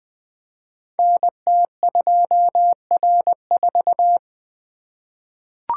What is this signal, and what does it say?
Morse code, heard as a keyed tone, about 20 wpm, 700 Hz. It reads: NT2R4
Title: Code Groups - 5 Letter:Numbers - 20wpm